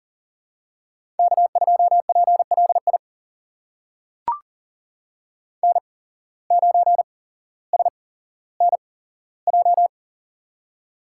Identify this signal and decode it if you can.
Morse code, heard as a keyed tone, about 40 wpm, spaced as for 17 wpm, 700 Hz. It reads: K2PLI N9SNJ
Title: USA Call Signs - 5 Characters - 8x Character Spacing - 40wpm